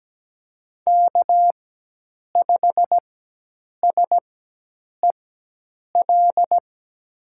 Morse code, heard as a keyed tone, about 17 wpm, spaced as for 8 wpm, 700 Hz. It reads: K5SEL
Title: USA Call Signs - 5 Characters - 4x Character Spacing - 17wpm